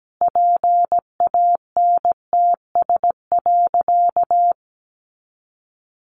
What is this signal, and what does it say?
Morse code, heard as a keyed tone, about 17 wpm, 700 Hz. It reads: PANTS.
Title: Additional Easy Phrases 17wpm